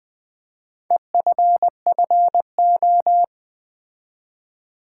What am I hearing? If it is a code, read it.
Morse code, heard as a keyed tone, about 20 wpm, 700 Hz. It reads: EFFO
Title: Build A Word - 2 Letters At A Time - 6-Letter Words - 20wpm